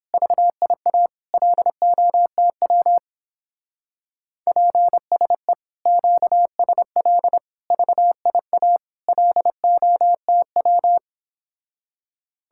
Morse code, heard as a keyed tone, about 31 wpm, 700 Hz. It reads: VIA LOTW PSE QSL VIA LOTW